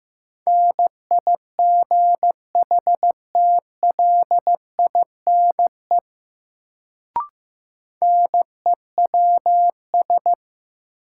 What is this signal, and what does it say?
Morse code, heard as a keyed tone, about 15 wpm, 700 Hz. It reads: NIGHTLINE NEWS
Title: Sets of 5 Words - New Word Drill - 15wpm